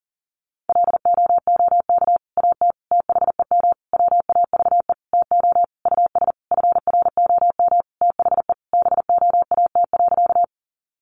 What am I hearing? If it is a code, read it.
Morse code, heard as a keyed tone, about 40 wpm, 700 Hz. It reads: LOOK AT THEM WAVE TO US FROM THE BOAT.
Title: Sentences from Top 1400 Words 40wpm